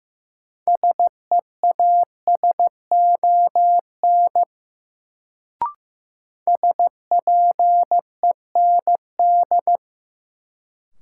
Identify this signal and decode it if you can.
Morse code, heard as a keyed tone, about 15 wpm, 700 Hz. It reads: SEASON SPEND
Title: Sets of 5 Words - New Word Drill - 15wpm